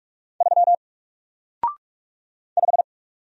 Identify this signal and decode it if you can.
Morse code, heard as a keyed tone, about 46 wpm, 700 Hz. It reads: 3 5
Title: Single Character (Letter:Number:Punctuation:Prosign) Alt 45wpm